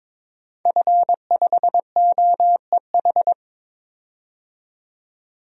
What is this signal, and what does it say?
Morse code, heard as a keyed tone, about 22 wpm, 700 Hz. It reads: F5OEH
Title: DXCC-100 Call Signs - 5 Characters - 22wpm